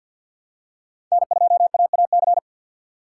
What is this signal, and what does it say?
Morse code, heard as a keyed tone, about 49 wpm, 700 Hz. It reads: N2AAC
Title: USA Call Signs - 5 Characters - 2x Character Spacing - 50wpm